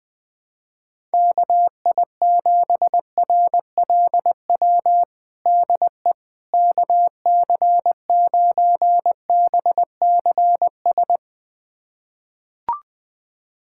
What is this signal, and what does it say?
Morse code, heard as a keyed tone, about 20 wpm, 700 Hz. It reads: KI7RLW DE KC9BCS
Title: USA Callsign DE Callsign - 6 Characters - 20wpm